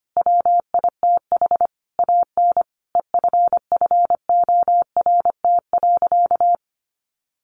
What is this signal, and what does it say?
Morse code, heard as a keyed tone, about 25 wpm, 700 Hz. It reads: WITH AN EFFORT.
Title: Sentences from Top 1500 Words 25wpm